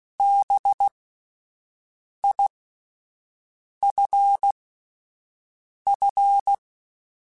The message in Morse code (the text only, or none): BIFF